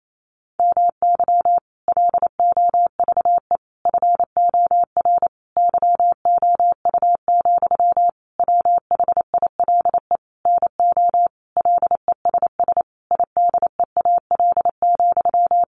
Morse code, heard as a keyed tone, about 28 wpm, 700 Hz. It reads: MY LOVE FOR YOU, WHILE NO LESS IDEAL,
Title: Sentences from Top 1400 Words 28wpm